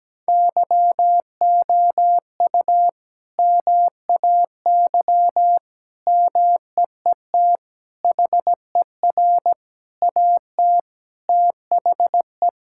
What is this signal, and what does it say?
Morse code, heard as a keyed tone, about 17 wpm, 700 Hz. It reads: YOU MAY MEET HER AT THE
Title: Sentences from Top 1600 Words 17wpm